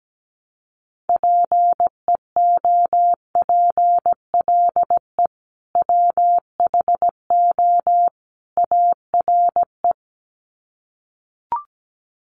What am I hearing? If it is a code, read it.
Morse code, heard as a keyed tone, about 17 wpm, 700 Hz. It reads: PEOPLE WHO ARE
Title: Sets of 3 Words 17wpm